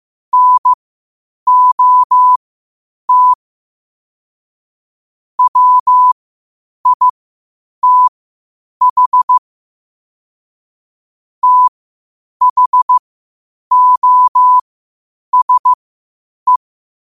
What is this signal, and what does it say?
Morse code, heard as a keyed tone, about 15 wpm, spaced as for 8 wpm, 1000 Hz. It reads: NOT WITH THOSE
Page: 8 wpm morse code quotes